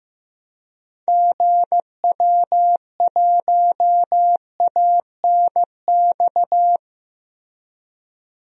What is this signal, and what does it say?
Morse code, heard as a keyed tone, about 15 wpm, 700 Hz. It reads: GW1ANX
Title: DXCC-100 Call Signs - 6 Characters - Repeated 3x - 15wpm